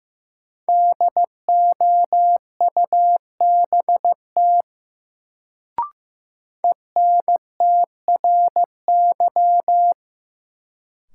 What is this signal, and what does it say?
Morse code, heard as a keyed tone, about 15 wpm, 700 Hz. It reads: DOUBT ENTRY
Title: Sentences from Top 1500 Words - New Word Drill - 15wpm